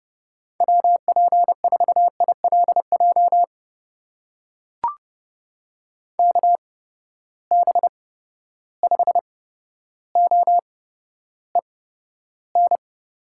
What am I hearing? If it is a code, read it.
Morse code, heard as a keyed tone, about 30 wpm, spaced as for 13 wpm, 700 Hz. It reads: WP4ILJ KB5OEN
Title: USA Call Signs - 8x Character Spacing - 30wpm